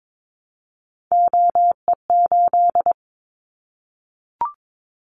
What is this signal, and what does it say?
Morse code, heard as a keyed tone, about 22 wpm, 700 Hz. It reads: OE8
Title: Callsign Prefixes for Top 100 DXCC - 22wpm